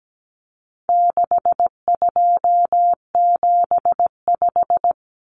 Morse code, heard as a keyed tone, about 17 wpm, 700 Hz. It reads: 6275
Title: Numbers - 4-Digit 17wpm